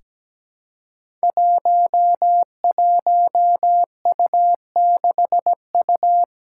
Morse code, heard as a keyed tone, about 17 wpm, 700 Hz. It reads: 11U6U
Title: Five Challenging Letter-Number Combinations 17wpm